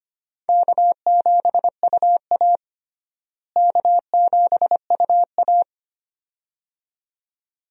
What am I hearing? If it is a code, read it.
Morse code, heard as a keyed tone, about 25 wpm, 700 Hz. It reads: K7UA K7UA
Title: USA Call Signs - 4 Characters - Repeated 2x - 25wpm